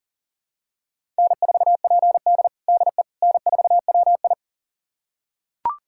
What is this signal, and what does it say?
Morse code, heard as a keyed tone, about 37 wpm, 700 Hz. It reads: N4PD DE N4WI
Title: USA Callsign DE Callsign - 4 Characters - 40wpm